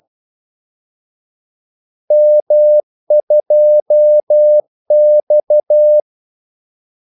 Morse code, heard as a keyed tone, about 12 wpm, 600 Hz. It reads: M2X
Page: 2 ekstra ordmellemrum mellem kaldesignalerne